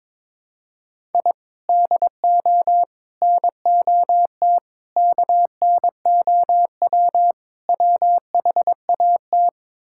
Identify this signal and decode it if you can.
Morse code, heard as a keyed tone, about 22 wpm, 700 Hz. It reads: I DO NOT KNOW WHAT
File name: practice / 22wpm / Build A Phrase - 5-Word Phrases - 22wpm